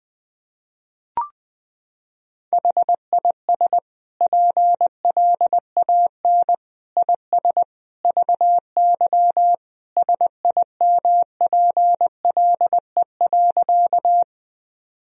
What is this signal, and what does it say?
Morse code, heard as a keyed tone, about 21 wpm, 700 Hz. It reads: HIS PLAN IS VY SIMPLE.
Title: Sentences from Top 500 Words and Abbreviations 20wpm